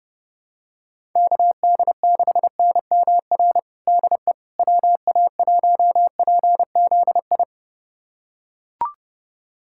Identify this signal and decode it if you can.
Morse code, heard as a keyed tone, about 30 wpm, 700 Hz. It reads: KD6NMR DE WA1PZI
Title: USA Callsign DE Callsign 30wpm